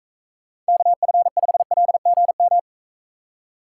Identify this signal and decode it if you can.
Morse code, heard as a keyed tone, about 43 wpm, 700 Hz. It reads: KF5LGM